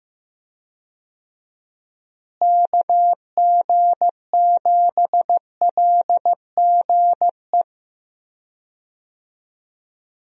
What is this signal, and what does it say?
Morse code, heard as a keyed tone, about 15 wpm, 700 Hz. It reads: KG7LGE
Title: USA Call Signs - 6 Characters - Repeated 3x - 15wpm